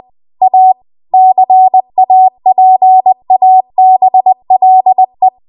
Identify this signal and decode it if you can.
Morse code, heard as a keyed tone, about 20 wpm, 750 Hz. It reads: A CAPABLE
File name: Archive / 20 WPM / 20WPM